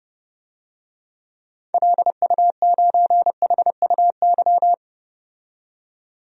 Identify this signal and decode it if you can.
Morse code, heard as a keyed tone, about 31 wpm, 700 Hz. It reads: LU9HUY
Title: DXCC-100 Call Signs - 6 Characters - 30wpm